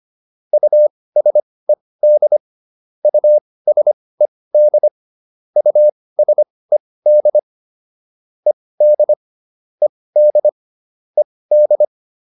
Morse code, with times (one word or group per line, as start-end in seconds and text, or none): 0.5-2.4 USED
3.0-4.9 USED
5.6-7.4 USED
8.5-9.1 ED
9.8-10.5 ED
11.2-11.8 ED